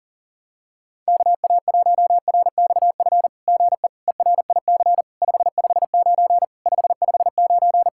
Morse code, heard as a keyed tone, about 41 wpm, 700 Hz. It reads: KA1RXF GE ERIC 559 559
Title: SOTA QSO Practice 40wpm